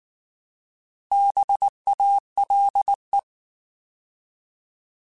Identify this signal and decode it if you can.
Morse code, heard as a keyed tone, about 19 wpm, 800 Hz. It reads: BALE
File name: Alphabetical / Alphabetical mp3s / Bs @18